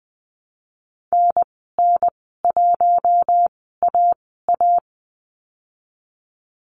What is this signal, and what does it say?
Morse code, heard as a keyed tone, about 19 wpm, spaced as for 14 wpm, 700 Hz. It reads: NN1AA
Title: 055 Lesson 10g - Letter r - Call Signs 20wpm